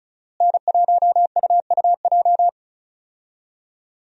Morse code, heard as a keyed tone, about 36 wpm, 700 Hz. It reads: N1UUJ
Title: USA Call Signs - 5 Characters - Repeated 3x - 35wpm